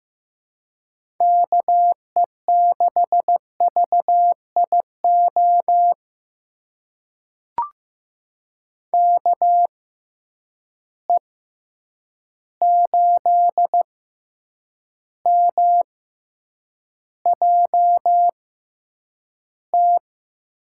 Morse code, heard as a keyed tone, about 15 wpm, spaced as for 8 wpm, 700 Hz. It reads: KE6VIO KE8MJT
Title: USA Call Signs - 6 Characters - 6x Character Spacing - 15wpm